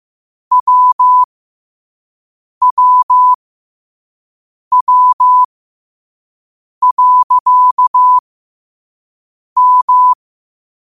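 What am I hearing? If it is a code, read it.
Morse code, heard as a keyed tone, about 15 wpm, spaced as for 5 wpm, 1000 Hz. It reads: WWW.M
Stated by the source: Morse quote of the day at 5 wpm.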